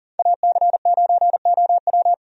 Morse code, heard as a keyed tone, about 40 wpm, 700 Hz. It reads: AC9OW
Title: USA Call Signs - 5 Characters - 40wpm